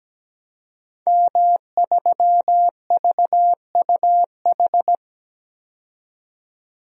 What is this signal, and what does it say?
Morse code, heard as a keyed tone, about 17 wpm, 700 Hz. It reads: M3VUH
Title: DXCC-100 Call Signs - 5 Characters - 17wpm